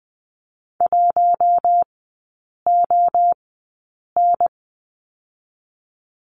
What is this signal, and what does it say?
Morse code, heard as a keyed tone, about 19 wpm, spaced as for 8 wpm, 690 Hz. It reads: 1ON
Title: 060 Lesson 11e - Letter h - Three Characters 20wpm